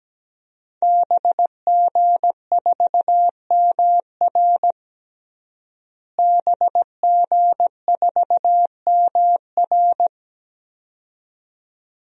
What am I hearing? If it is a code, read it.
Morse code, heard as a keyed tone, about 17 wpm, 700 Hz. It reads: BG4MR BG4MR
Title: DXCC-100 Call Signs - 5 Characters - Repeated 2x - 17wpm